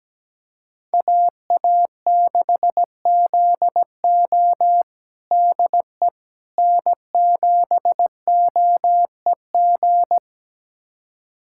Morse code, heard as a keyed tone, about 17 wpm, 700 Hz. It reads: AA6ZO DE N7OEG
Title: USA Callsign DE Callsign - 5 Characters - 17wpm